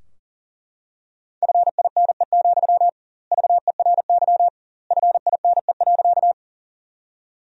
Morse code, heard as a keyed tone, about 40 wpm, 700 Hz. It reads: FINE, VERY FINE.
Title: 2x Word-Spacing - Sentences from Top 500 Words - 40wpm